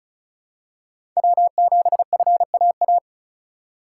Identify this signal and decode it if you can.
Morse code, heard as a keyed tone, about 34 wpm, 700 Hz. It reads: W7FAA
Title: USA Call Signs - 5 Characters - 4x Character Spacing - 35wpm